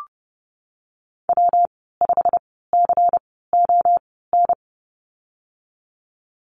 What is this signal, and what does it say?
Morse code, heard as a keyed tone, about 30 wpm, spaced as for 17 wpm, 700 Hz. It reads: W5CON
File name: Lesson 19g - Letter w - Call Signs 30wpm